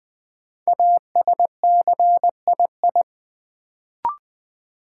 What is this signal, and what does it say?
Morse code, heard as a keyed tone, about 20 wpm, 700 Hz. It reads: ASCII
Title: Sets of 5 Words - New Word Drill - 20wpm